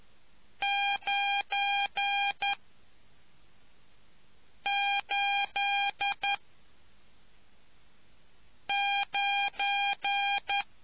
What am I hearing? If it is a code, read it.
Morse code, heard as a keyed tone, about 11 wpm, 800 Hz. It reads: 9 8 9